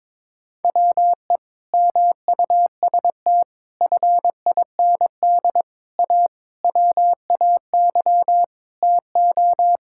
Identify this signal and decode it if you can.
Morse code, heard as a keyed tone, about 22 wpm, 700 Hz. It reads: WE MUST FIND A WAY TO
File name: practice / 22wpm / Sentences with 4-Word Phrases - 22wpm